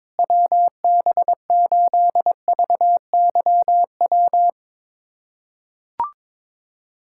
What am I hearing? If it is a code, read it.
Morse code, heard as a keyed tone, about 22 wpm, 700 Hz. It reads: WB8VYW
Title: USA Callsign DE Callsign 22wpm